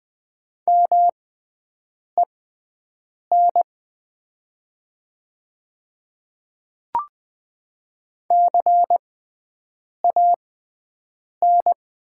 Morse code, heard as a keyed tone, about 20 wpm, spaced as for 7 wpm, 700 Hz. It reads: MEN CAN
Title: 6x Character Spacing - Top 200 Words - 20wpm